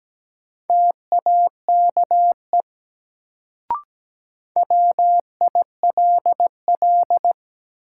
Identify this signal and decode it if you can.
Morse code, heard as a keyed tone, about 17 wpm, 700 Hz. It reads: TAKE WILL
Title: Sets of 2 Words - New Word Drill - 17wpm